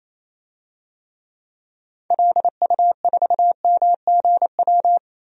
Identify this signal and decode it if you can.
Morse code, heard as a keyed tone, about 28 wpm, 700 Hz. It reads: LU4MGW
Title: DXCC-100 Call Signs - 6 Characters - 28wpm